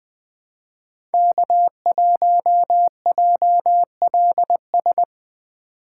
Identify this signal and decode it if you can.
Morse code, heard as a keyed tone, about 21 wpm, 700 Hz. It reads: K1JLS
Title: USA Call Signs - 5 Characters - 4x Character Spacing - 20wpm